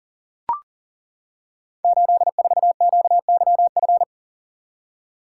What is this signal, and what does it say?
Morse code, heard as a keyed tone, about 41 wpm, 690 Hz. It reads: 84QYF
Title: Code Groups - 5 Letter:Numbers - 40wpm